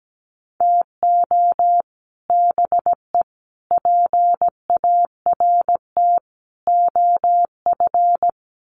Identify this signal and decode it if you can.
Morse code, heard as a keyed tone, about 17 wpm, 700 Hz. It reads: TO BE PART OF
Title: Sets of 5 Words Encore2 17wpm